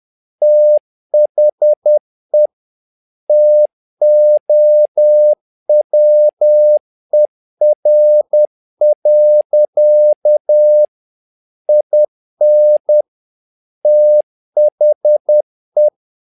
Morse code, heard as a keyed tone, about 10 wpm, 600 Hz. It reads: THE TOWER. IN THE
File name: learning center / tower trouble / tower trouble 10wpm